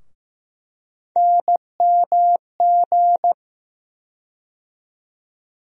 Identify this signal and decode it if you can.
Morse code, heard as a keyed tone, about 15 wpm, 700 Hz. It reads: NMG
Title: USA Call Sign Postfixes - 3 Letters 15wpm